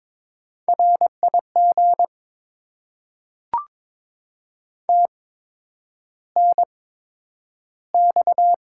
Morse code, heard as a keyed tone, about 22 wpm, spaced as for 9 wpm, 700 Hz. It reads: RIG TNX
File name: Top 20 QSO Elements - 8x Character Spacing - 22wpm